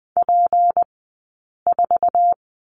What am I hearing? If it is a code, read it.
Morse code, heard as a keyed tone, about 20 wpm, spaced as for 8 wpm, 700 Hz. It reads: P4
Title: 211 Lesson 34d - Number 8 - Two Characters 20wpm